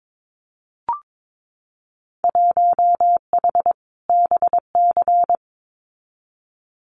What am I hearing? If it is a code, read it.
Morse code, heard as a keyed tone, about 22 wpm, 700 Hz. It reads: 1H BC
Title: Winter Field Day Exchanges 22wpm